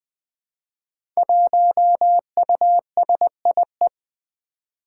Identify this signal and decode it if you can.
Morse code, heard as a keyed tone, about 20 wpm, 700 Hz. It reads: 1USIE
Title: Five Challenging Letter-Number Combinations 20wpm